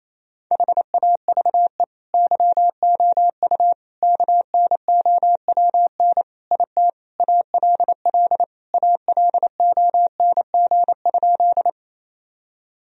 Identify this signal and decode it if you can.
Morse code, heard as a keyed tone, about 28 wpm, 700 Hz. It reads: HAVE YOU KNOWN IT ALL ALONG?
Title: Sentences from Top 1000 Words - Review Part 3 - 28wpm